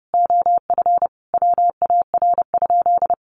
Morse code, heard as a keyed tone, about 30 wpm, 700 Hz. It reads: OF WAR?
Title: Sentences from Top 1300 Words 30wpm